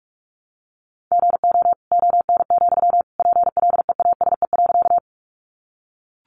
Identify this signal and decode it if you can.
Morse code, heard as a keyed tone, about 44 wpm, 700 Hz. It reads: GO ON, PLEASE.
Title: Sentences from Top 1500 Words 45wpm